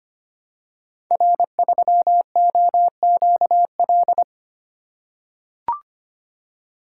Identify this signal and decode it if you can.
Morse code, heard as a keyed tone, about 25 wpm, 700 Hz. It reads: R3OQL
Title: Code Groups - 5 Letter:Numbers - 25wpm